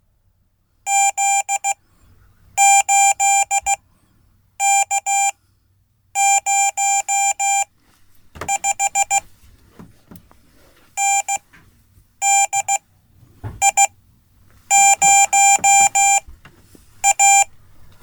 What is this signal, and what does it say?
Morse code, heard as a keyed tone, about 16 wpm, spaced as for 8 wpm, 790 Hz. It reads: Z8K05 NDI0A
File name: Code / 5min-a